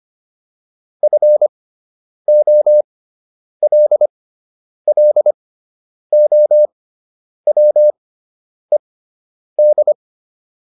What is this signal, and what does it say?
Morse code, heard as a keyed tone, about 25 wpm, spaced as for 9 wpm, 600 Hz. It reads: FOLLOWED